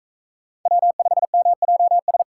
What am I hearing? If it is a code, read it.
Morse code, heard as a keyed tone, about 44 wpm, 700 Hz. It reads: W5MJS